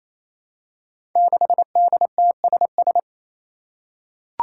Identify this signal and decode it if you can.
Morse code, heard as a keyed tone, about 29 wpm, 700 Hz. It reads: 6DTSS
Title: Five Challenging Letter-Number Combinations 28wpm